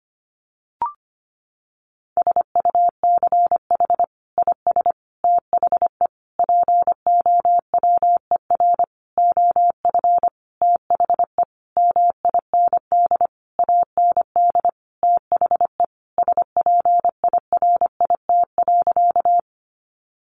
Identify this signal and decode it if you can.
Morse code, heard as a keyed tone, about 25 wpm, 700 Hz. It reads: SUCH IS THE POWER OF THE MIND AND THE SPIRIT.
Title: Sentences from Top 1400 Words 25wpm